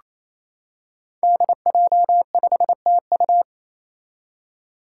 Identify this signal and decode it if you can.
Morse code, heard as a keyed tone, about 28 wpm, 700 Hz. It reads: DJ5TU
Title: Five Challenging Letter-Number Combinations 28wpm